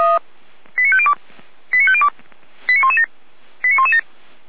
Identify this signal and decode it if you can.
Five-tone selective call (ZVEI), heard as quick runs of five tones, tones of about 70 ms. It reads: 89502 89502 89117 89117